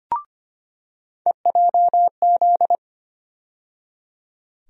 Morse code, heard as a keyed tone, about 25 wpm, 700 Hz. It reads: EJZ
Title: USA Call Sign Postfixes - 3 Letters 25wpm